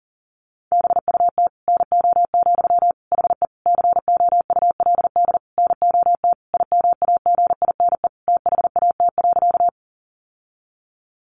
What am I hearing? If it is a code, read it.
Morse code, heard as a keyed tone, about 40 wpm, 700 Hz. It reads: BUT NO, HE COULD NOT IMAGINE THAT.
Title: Sentences from Top 1400 Words 40wpm